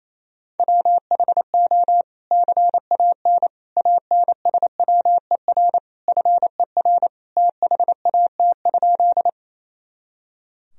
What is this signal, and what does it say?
Morse code, heard as a keyed tone, about 28 wpm, 700 Hz. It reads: WHO CAN ANSWER FER THAT?
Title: Sentences from Top 300 Words and Abbreviations 28wpm